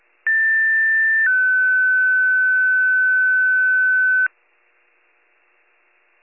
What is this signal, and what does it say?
Two-tone page, heard as a one-second tone, then a three-second tone, about 1800 then 1500 Hz. It is heard from 0.3 to 4.3 s.